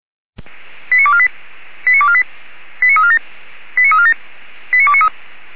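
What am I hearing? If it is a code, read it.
Five-tone selective call (ZVEI), heard as quick runs of five tones, tones of about 70 ms. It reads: 89248 89248 89347 89347 89293
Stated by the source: Received on an SDR; ZVEI five-tone call.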